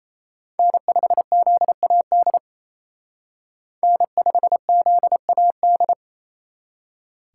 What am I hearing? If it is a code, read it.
Morse code, heard as a keyed tone, about 35 wpm, 700 Hz. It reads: N5ZAD N5ZAD
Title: Speed Racing - USA 1x3 Call Signs 28wpm